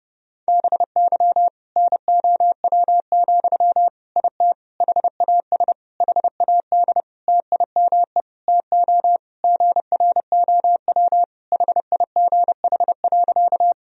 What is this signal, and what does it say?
Morse code, heard as a keyed tone, about 32 wpm, 700 Hz. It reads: BY NOW, IT HAS HAD TIME TO GROW HIGH.
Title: Sentences from Top 1000 Words - Review Part 1 - 30wpm